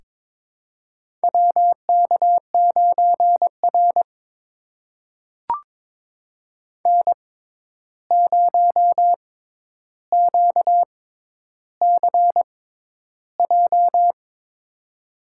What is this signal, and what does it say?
Morse code, heard as a keyed tone, about 21 wpm, spaced as for 10 wpm, 700 Hz. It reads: WK9R N0QCJ
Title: USA Call Signs - 6x Character Spacing - 22wpm